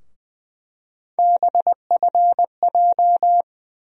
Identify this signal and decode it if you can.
Morse code, heard as a keyed tone, about 20 wpm, 700 Hz. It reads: BFJ
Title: USA Call Sign Postfixes - 3 Letters 20wpm